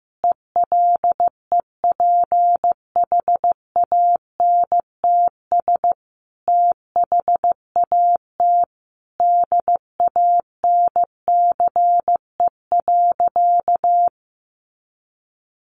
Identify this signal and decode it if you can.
Morse code, heard as a keyed tone, about 15 wpm, 700 Hz. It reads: ELEPHANTS THAT DANCE.
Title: Easy Phrases 15wpm